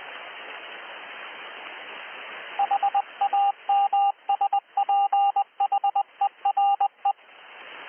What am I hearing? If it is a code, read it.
Morse code, heard as a keyed tone, about 20 wpm, 800 Hz. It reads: HAMSPHERE
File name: hs beacon